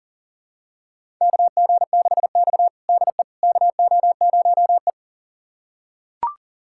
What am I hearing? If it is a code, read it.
Morse code, heard as a keyed tone, about 40 wpm, 700 Hz. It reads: KG6X DE KO0E
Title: USA Callsign DE Callsign - 4 Characters - 40wpm